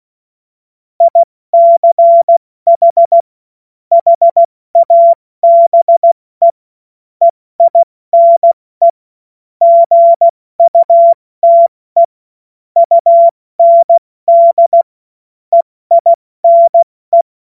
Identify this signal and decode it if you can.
Morse code, heard as a keyed tone, about 16 wpm, 680 Hz. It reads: ICH HABE EINE GUTE UND EINE
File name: Morsekurs / Witze / Witz 13. Juni 2023 70 F80